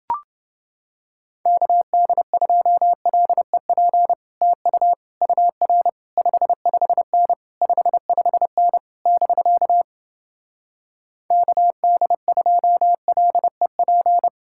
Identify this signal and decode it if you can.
Morse code, heard as a keyed tone, about 28 wpm, 700 Hz. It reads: KD2LEP TU UR 55N 55N <BK> KD2LEP